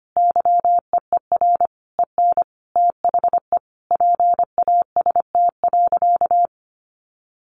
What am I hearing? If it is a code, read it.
Morse code, heard as a keyed tone, about 25 wpm, 700 Hz. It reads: YEER EN THE PAST.
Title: A Sound Of Thunder 25wpm